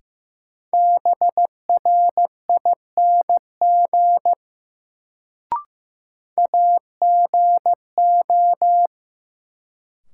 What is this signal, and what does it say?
Morse code, heard as a keyed tone, about 15 wpm, 700 Hz. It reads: BRING AGO
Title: Sentences from Top 500 Words - New Word Drill - 15wpm